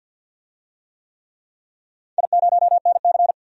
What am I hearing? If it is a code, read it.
Morse code, heard as a keyed tone, about 51 wpm, 700 Hz. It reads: I0NC
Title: DXCC-100 Call Signs - 4 Characters - 50wpm